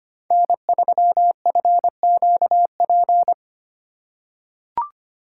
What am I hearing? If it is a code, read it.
Morse code, heard as a keyed tone, about 25 wpm, 700 Hz. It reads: N3FQP
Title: USA Call Signs - 5 Characters - 2x Character Spacing - 25wpm